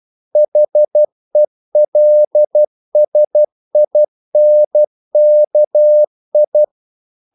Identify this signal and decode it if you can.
Morse code, heard as a keyed tone, about 12 wpm, 600 Hz. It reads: HELSINKI
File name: CW / Practice / NationalCapitals12